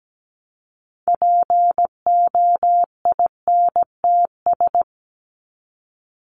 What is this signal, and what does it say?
Morse code, heard as a keyed tone, about 17 wpm, 700 Hz. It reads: POINTS
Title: ES:S Suffix - Top 100 Words with Base Words 17wpm